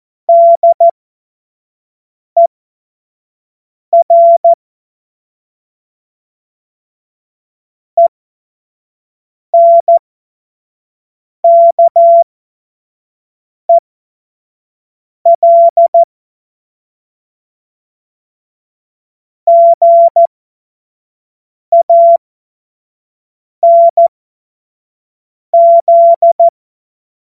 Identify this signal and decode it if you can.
Morse code, heard as a keyed tone, about 14 wpm, spaced as for 5 wpm, 680 Hz. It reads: DER ENKEL GANZ
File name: Morsekurs / Witze / Witz 26. Oktober 2021, 25 F70